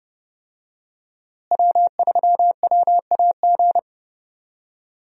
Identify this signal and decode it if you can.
Morse code, heard as a keyed tone, about 29 wpm, 700 Hz. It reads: W3WAG
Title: USA Call Signs - 5 Characters - Repeated 2x - 30wpm